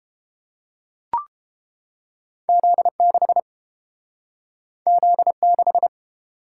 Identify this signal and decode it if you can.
Morse code, heard as a keyed tone, about 30 wpm, 700 Hz. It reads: Z6 Z6